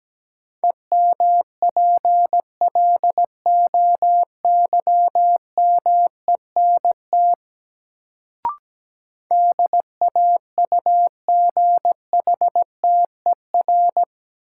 Morse code, heard as a keyed tone, about 17 wpm, 700 Hz. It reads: EMPLOYMENT DAUGHTER
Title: Sentences from Top 1500 Words - New Word Drill Alt - 17wpm